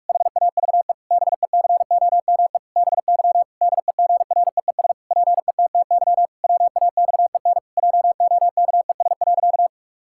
Morse code, heard as a keyed tone, about 46 wpm, 700 Hz. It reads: HAVE BECOME BY DEGREES PETTY WAXEN JOKES.
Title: Pangram Challenge v2 - 45wpm